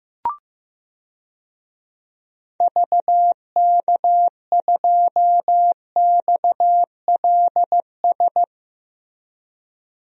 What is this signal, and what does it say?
Morse code, heard as a keyed tone, about 15 wpm, 700 Hz. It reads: VK2XLS
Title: DXCC-100 Call Signs - 6 Characters - Repeated 3x - 15wpm